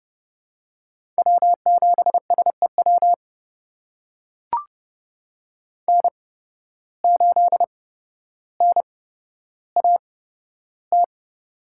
Morse code, heard as a keyed tone, about 30 wpm, spaced as for 13 wpm, 700 Hz. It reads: W7SEW N8NAT
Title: USA Call Signs - 5 Characters - 8x Character Spacing - 30wpm